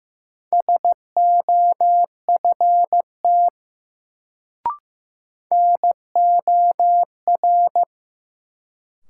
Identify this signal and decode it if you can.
Morse code, heard as a keyed tone, about 15 wpm, 700 Hz. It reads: SOFT NOR
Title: Sentences from Top 1500 Words - New Word Drill - 15wpm